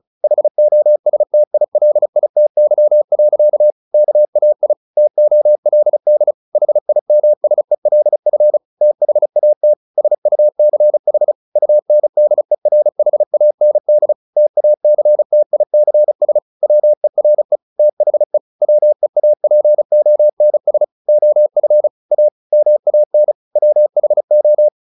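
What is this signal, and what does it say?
Morse code, heard as a keyed tone, about 36 wpm, 600 Hz. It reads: HOSTILITY. KAI TOLD HIMSELF THAT SUCH UNDERHAND TACTICS WERE THE WEAPONS OF A MAN WHO
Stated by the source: Morse practice at 35 wpm from CW med Gnister, sending Ken Follett's novel Never in English.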